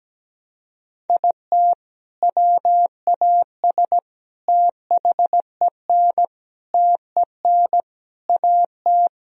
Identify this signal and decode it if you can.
Morse code, heard as a keyed tone, about 17 wpm, 700 Hz. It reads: IT WAS THEN TEN AT
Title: Sentences from Top 1000 Words - Review Part 2 - 17wpm